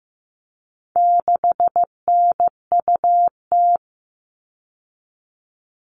Morse code, heard as a keyed tone, about 15 wpm, 700 Hz. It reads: 6NUT
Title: All Cut Numbers - 4-Digit 15wpm